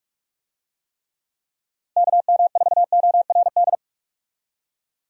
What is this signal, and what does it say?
Morse code, heard as a keyed tone, about 46 wpm, 700 Hz. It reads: KM4ORD
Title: USA Call Signs - 6 Characters - 45wpm